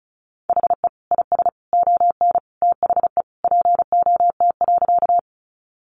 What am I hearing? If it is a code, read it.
Morse code, heard as a keyed tone, about 34 wpm, 700 Hz. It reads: HE IS ON THE POT.
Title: Sentences from Top 1000 Words 35wpm